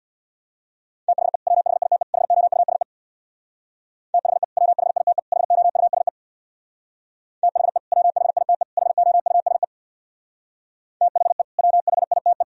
Morse code, heard as a keyed tone, about 64 wpm, 700 Hz. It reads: THE WHITE HOUSE THE WHITE HOUSE THE WHITE HOUSE THE WHITE